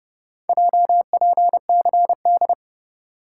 Morse code, heard as a keyed tone, about 30 wpm, 700 Hz. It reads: JPCD